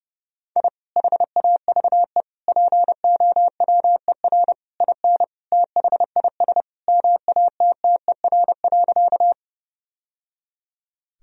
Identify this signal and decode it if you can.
Morse code, heard as a keyed tone, about 32 wpm, 700 Hz. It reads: I HAVE POWER IN THIS MATTER.
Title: Sentences from Top 1000 Words - Review Part 2 - 30wpm